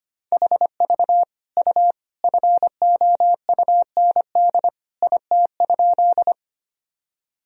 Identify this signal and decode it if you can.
Morse code, heard as a keyed tone, about 25 wpm, 700 Hz. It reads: HV U FOUND IT?
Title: Sentences from Top 300 Words and Abbreviations 25wpm